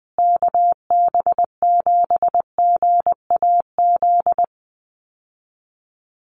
Morse code, heard as a keyed tone, about 19 wpm, 700 Hz. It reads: KB7GAZ
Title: USA Call Signs Repeated 2x 20wpm